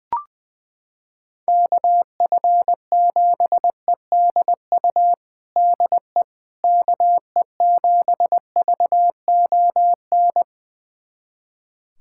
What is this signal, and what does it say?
Morse code, heard as a keyed tone, about 21 wpm, 700 Hz. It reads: KF7EDU DE KE7VON
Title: USA Callsign DE Callsign - 6 Characters - 20wpm